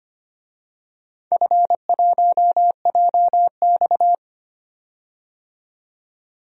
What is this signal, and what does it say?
Morse code, heard as a keyed tone, about 25 wpm, 700 Hz. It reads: F1JX
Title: DXCC-100 Call Signs - 4 Characters - 25wpm